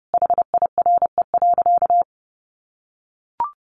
Morse code, heard as a keyed tone, about 30 wpm, 700 Hz. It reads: HIRE.
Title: Sentences from Top 1400 Words 30wpm